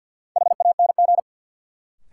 Morse code, heard as a keyed tone, about 50 wpm, 700 Hz. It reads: HANG
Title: Sentences from Top 1600 Words - New Word Drill - 50wpm